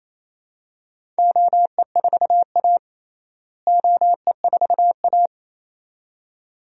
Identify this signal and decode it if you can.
Morse code, heard as a keyed tone, about 28 wpm, 700 Hz. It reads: OE4A OE4A
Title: DXCC-100 Call Signs - 4 Characters - Repeated 2x - 28wpm